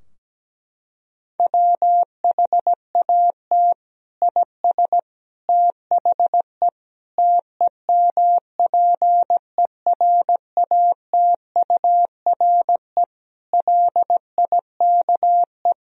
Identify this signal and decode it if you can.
Morse code, heard as a keyed tone, about 17 wpm, 700 Hz. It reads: WHAT IS THE TEMPERATURE LIKE
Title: Sentences from Top 1000 Words - Review Part 4 - 17wpm